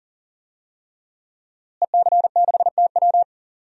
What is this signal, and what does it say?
Morse code, heard as a keyed tone, about 37 wpm, 700 Hz. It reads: EC6TW
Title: DXCC-100 Call Signs - 5 Characters - Repeated 3x - 40wpm